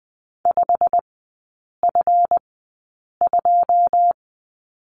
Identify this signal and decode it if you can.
Morse code, heard as a keyed tone, about 20 wpm, spaced as for 8 wpm, 700 Hz. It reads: 5F2
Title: 138 Lesson 23e - Letter f - Three Characters 20wpm